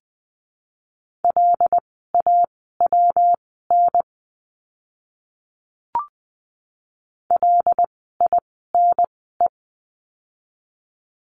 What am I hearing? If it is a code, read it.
Morse code, heard as a keyed tone, about 20 wpm, spaced as for 15 wpm, 700 Hz. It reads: LAWN LINE
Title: 115 Lesson 19f - Letter w - Words 20wpm